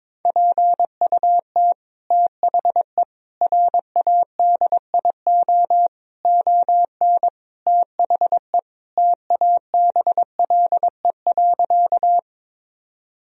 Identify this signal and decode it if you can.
Morse code, heard as a keyed tone, about 22 wpm, 700 Hz. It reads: PUT THE RADIO ON THE TABLE.
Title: Sentences from Top 1000 Words - Review Part 3 - 22wpm